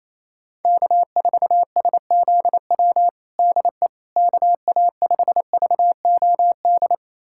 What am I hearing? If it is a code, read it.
Morse code, heard as a keyed tone, about 28 wpm, 700 Hz. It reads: K4SZW DE KA5VOD
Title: USA Callsign DE Callsign 28wpm